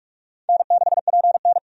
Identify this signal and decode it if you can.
Morse code, heard as a keyed tone, about 46 wpm, 700 Hz. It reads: N6PN
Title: USA Call Signs - 4 Characters - 45wpm